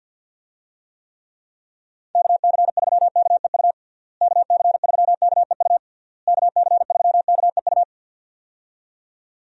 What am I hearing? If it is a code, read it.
Morse code, heard as a keyed tone, about 51 wpm, 700 Hz. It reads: KC3KEU KC3KEU KC3KEU
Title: USA Call Signs - 6 Characters - Repeated 3x - 50wpm